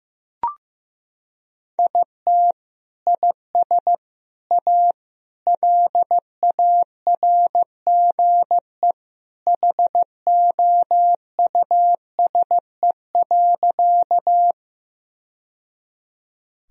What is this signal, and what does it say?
Morse code, heard as a keyed tone, about 15 wpm, 700 Hz. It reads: IT IS A LARGE HOUSE.
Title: Sentences from Top 1000 Words - Review Part 1 - 15wpm